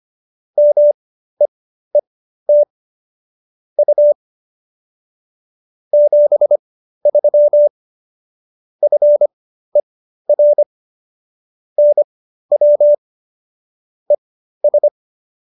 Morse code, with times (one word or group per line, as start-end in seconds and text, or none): 0.6-2.6 MEET
3.8-4.1 U
5.9-7.7 73
8.8-10.6 FER
11.8-12.9 NW
14.1-14.9 ES